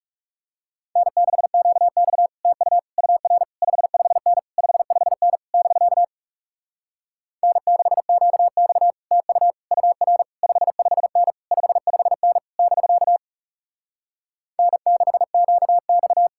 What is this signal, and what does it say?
Morse code, heard as a keyed tone, about 38 wpm, 700 Hz. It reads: N6QX TU UR 55N 55N <BK> N6QX TU UR 55N 55N <BK> N6QX
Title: Speed-Racing Common QSO Phrases - 30wpm